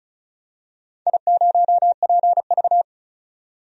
Morse code, heard as a keyed tone, about 36 wpm, 700 Hz. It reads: I0PV
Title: DXCC-100 Call Signs - 4 Characters - 35wpm